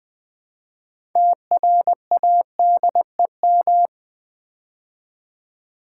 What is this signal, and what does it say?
Morse code, heard as a keyed tone, about 20 wpm, 700 Hz. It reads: TRADEM
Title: Build A Word - 3 Letters At A Time - 9-Letter Words - 20wpm